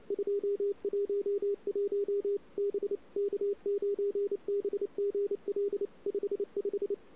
Morse code, heard as a keyed tone, about 31 wpm, 400 Hz. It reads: 211 B K9BGL 55
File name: CW / 2008-ss-cw-k5tr-op-n5rz -00:00